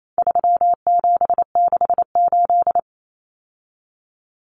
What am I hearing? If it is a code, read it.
Morse code, heard as a keyed tone, about 28 wpm, 700 Hz. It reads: 3768
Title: Numbers - 4-Digit 28wpm